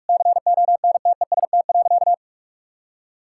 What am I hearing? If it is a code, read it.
Morse code, heard as a keyed tone, about 45 wpm, 700 Hz. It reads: CONTEST.
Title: Sentences from Top 1600 Words 45wpm